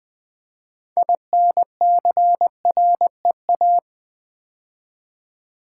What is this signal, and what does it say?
Morse code, heard as a keyed tone, about 20 wpm, 700 Hz. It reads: INCREA
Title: Build A Word - 6 Letters At A Time - 12-Letter Words - 20wpm